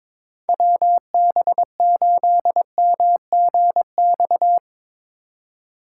Morse code, heard as a keyed tone, about 22 wpm, 700 Hz. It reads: WB8MGX
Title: USA Call Signs - 6 Characters - 3x Character Spacing - 22wpm